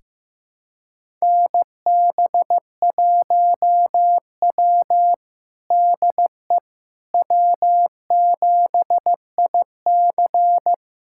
Morse code, heard as a keyed tone, about 15 wpm, 700 Hz. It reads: NB1W DE W7IC
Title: USA Callsign DE Callsign - 4 Characters - 15wpm